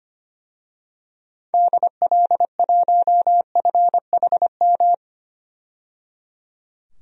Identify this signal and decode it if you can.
Morse code, heard as a keyed tone, about 25 wpm, 700 Hz. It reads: DL1FHM
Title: DXCC-100 Call Signs - 6 Characters - 25wpm